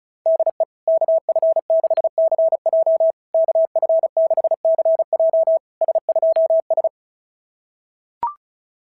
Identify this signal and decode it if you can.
Morse code, heard as a keyed tone, about 36 wpm, 690 Hz. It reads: DE KF6CJ KF6CJ S2S
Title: SOTA QSO Practice 35wpm